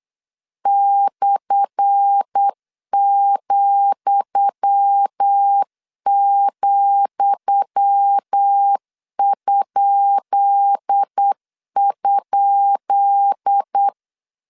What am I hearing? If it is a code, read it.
Morse code, heard as a keyed tone, about 8 wpm, 800 Hz. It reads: /,,??
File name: LEZIONI / 10 SIGNS / 10D COMA 8wpm 3749